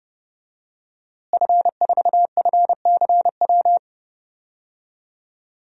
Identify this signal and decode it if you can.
Morse code, heard as a keyed tone, about 32 wpm, 700 Hz. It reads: F4FCW
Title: DXCC-100 Call Signs - 5 Characters - Repeated 3x - 30wpm